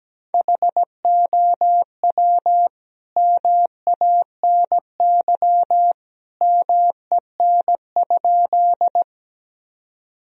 Morse code, heard as a keyed tone, about 17 wpm, 700 Hz. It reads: HOW MANY MEN?
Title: Sentences from Top 1000 Words - Review Part 1 - 17wpm